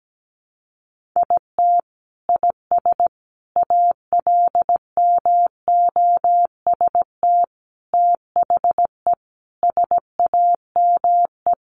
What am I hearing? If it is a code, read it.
Morse code, heard as a keyed tone, about 17 wpm, 700 Hz. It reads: IT IS ALMOST THE SAME
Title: Sentences from Top 1400 Words 17wpm